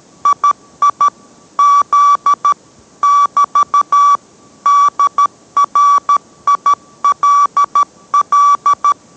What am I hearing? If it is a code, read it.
Morse code, heard as a keyed tone, about 13 wpm, 1150 Hz. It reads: II Z <BT> DRILL